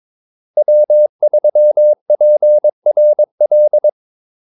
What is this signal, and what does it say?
Morse code, heard as a keyed tone, about 22 wpm, 600 Hz. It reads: W3PRL
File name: Callsigns-22wpm